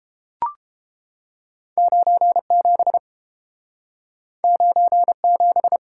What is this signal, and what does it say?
Morse code, heard as a keyed tone, about 30 wpm, 700 Hz. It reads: 97 97